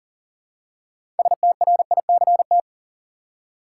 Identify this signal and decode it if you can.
Morse code, heard as a keyed tone, about 40 wpm, 700 Hz. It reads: STRICT
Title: Six Letter Words - 40wpm